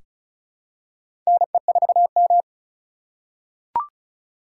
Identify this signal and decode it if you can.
Morse code, heard as a keyed tone, about 36 wpm, 700 Hz. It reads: NE4M
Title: K1USN SST QSO Practice - 35wpm